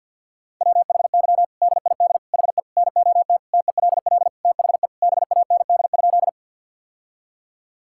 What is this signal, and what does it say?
Morse code, heard as a keyed tone, about 52 wpm, 700 Hz. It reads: WHY DID HE NOT TELL THE BAND?
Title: Sentences from Top 1600 Words 50wpm